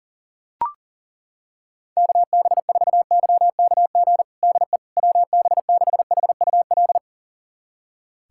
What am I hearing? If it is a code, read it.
Morse code, heard as a keyed tone, about 38 wpm, 700 Hz. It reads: KB4YKG DE WB6HUL
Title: USA Callsign DE Callsign 40wpm